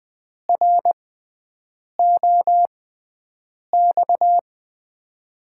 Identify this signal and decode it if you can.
Morse code, heard as a keyed tone, about 20 wpm, spaced as for 7 wpm, 700 Hz. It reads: ROX
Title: USA Call Signs - 5 Characters - 6x Character Spacing - 20wpm